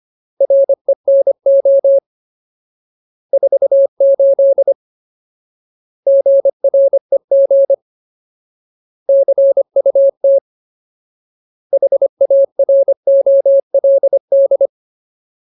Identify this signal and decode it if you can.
Morse code, heard as a keyed tone, about 25 wpm, 550 Hz. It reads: RENO 48 GREG CUT HAROLD
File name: Ultimate Word List / 25WPM Ultimate